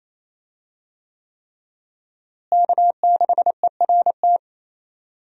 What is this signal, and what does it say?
Morse code, heard as a keyed tone, about 27 wpm, 700 Hz. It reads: K6ERT
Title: USA Call Signs - 5 Characters - Repeated 3x - 28wpm